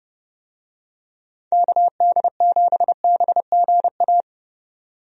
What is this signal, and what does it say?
Morse code, heard as a keyed tone, about 30 wpm, 700 Hz. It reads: KD7BGA
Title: USA Call Signs - 6 Characters - 30wpm